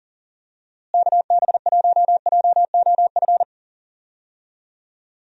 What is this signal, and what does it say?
Morse code, heard as a keyed tone, about 38 wpm, 700 Hz. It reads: KB1JOF
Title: USA Call Signs - 6 Characters - Repeated 2x - 40wpm